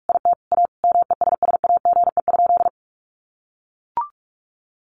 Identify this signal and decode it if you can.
Morse code, heard as a keyed tone, about 44 wpm, 700 Hz. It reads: IT A MESSAGE?
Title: Sentences from Top 1100 Words 45wpm